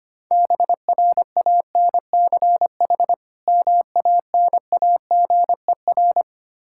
Morse code, heard as a keyed tone, about 25 wpm, 700 Hz. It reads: BRANCH MANAGER
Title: Practice QSO - Exchange 3 - More Info - 25wpm